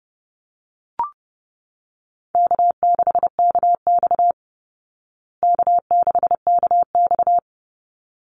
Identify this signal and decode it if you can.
Morse code, heard as a keyed tone, about 28 wpm, 700 Hz. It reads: K6KX K6KX
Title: Easy Call Signs Repeated 2x - 30wpm